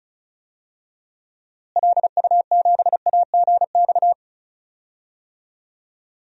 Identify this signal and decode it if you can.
Morse code, heard as a keyed tone, about 36 wpm, 700 Hz. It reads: LU7AGX
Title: DXCC-100 Call Signs - 6 Characters - Repeated 3x - 35wpm